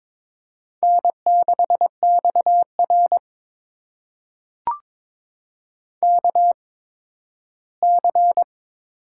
Morse code, heard as a keyed tone, about 22 wpm, spaced as for 13 wpm, 700 Hz. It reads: N6XR KC
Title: USA Call Signs - 8x Character Spacing - 22wpm